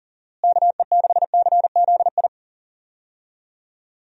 Morse code, heard as a keyed tone, about 40 wpm, 700 Hz. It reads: KE6CZI
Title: USA Call Signs - 6 Characters - Repeated 2x - 40wpm